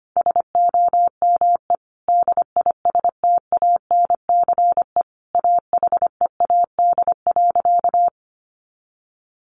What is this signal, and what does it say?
Morse code, heard as a keyed tone, about 25 wpm, 700 Hz. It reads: SOME DISTANCE AHEAD.
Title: Sentences from Top 1400 Words 25wpm